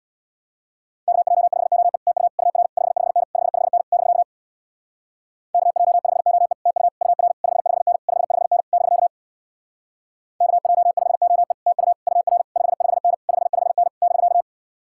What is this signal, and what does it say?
Morse code, heard as a keyed tone, about 64 wpm, 700 Hz. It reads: KJ5OE TU UR 55N 55N <BK> KJ5OE TU UR 55N 55N <BK> KJ5OE TU UR 55N 55N <BK>